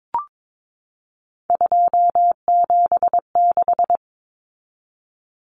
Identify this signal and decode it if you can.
Morse code, heard as a keyed tone, about 22 wpm, 700 Hz. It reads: 276
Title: Numbers - 3-Digit 22wpm